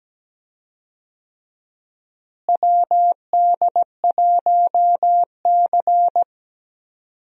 Morse code, heard as a keyed tone, about 17 wpm, 700 Hz. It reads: WD1C
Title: USA Call Signs - 4 Characters - Repeated 2x - 17wpm